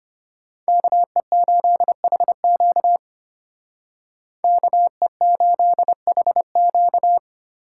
Morse code, heard as a keyed tone, about 32 wpm, 700 Hz. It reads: KE8HQ KE8HQ